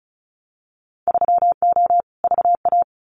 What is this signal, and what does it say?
Morse code, heard as a keyed tone, about 34 wpm, 700 Hz. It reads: 3O VA
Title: Winter Field Day Exchanges 35wpm